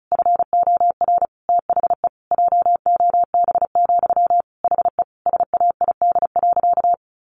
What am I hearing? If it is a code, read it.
Morse code, heard as a keyed tone, about 34 wpm, 700 Hz. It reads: FOR THE JOB, HE SAID.
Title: Sentences from Top 1500 Words 35wpm